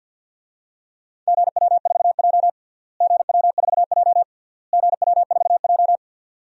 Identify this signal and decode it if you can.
Morse code, heard as a keyed tone, about 50 wpm, 700 Hz. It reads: GW4J GW4J GW4J
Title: DXCC-100 Call Signs - 4 Characters - Repeated 3x - 50wpm